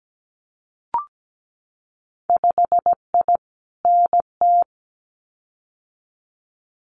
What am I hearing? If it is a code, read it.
Morse code, heard as a keyed tone, about 17 wpm, 700 Hz. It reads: 5I NT
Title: Winter Field Day Exchanges Encore 17wpm